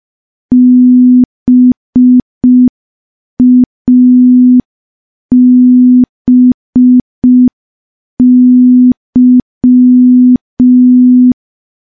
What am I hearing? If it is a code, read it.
Morse code, heard as a keyed tone, about 5 wpm, 250 Hz. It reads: BABY
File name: morse / morse-training-05WPM